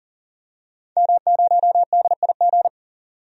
Morse code, heard as a keyed tone, about 40 wpm, 700 Hz. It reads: M0DIG
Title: DXCC-100 Call Signs - 5 Characters - 40wpm